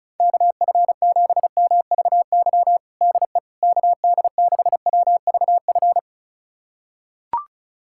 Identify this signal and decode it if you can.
Morse code, heard as a keyed tone, about 34 wpm, 700 Hz. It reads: KF7MVY DE KD6WVF
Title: USA Callsign DE Callsign - 6 Characters - 35wpm